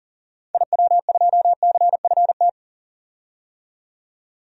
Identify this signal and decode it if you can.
Morse code, heard as a keyed tone, about 40 wpm, 700 Hz. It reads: IW2CFT
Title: DXCC-100 Call Signs - 6 Characters - 40wpm